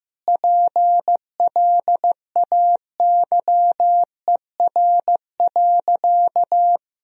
Morse code, heard as a keyed tone, about 15 wpm, 700 Hz. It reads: PLAYER.
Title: Sentences from Top 1000 Words - Review Part 4 - 15wpm